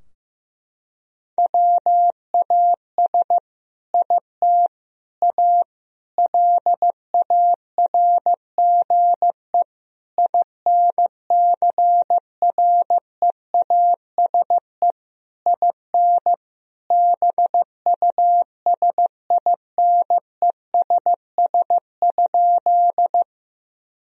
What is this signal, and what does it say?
Morse code, heard as a keyed tone, about 16 wpm, 700 Hz. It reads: WAS IT A LARGE INCREASE IN BUSINESS?
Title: Sentences from Top 1000 Words - Review Part 4 - 15wpm